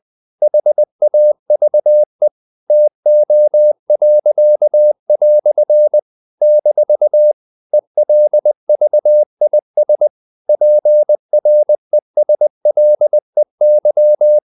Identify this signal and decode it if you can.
Morse code, heard as a keyed tone, about 20 wpm, 600 Hz. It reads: HAVE TO." - ELVIS PRESLEY
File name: CW / Practice / EnglishQuotations0320